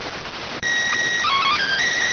SSTV (slow-scan television), PD 290.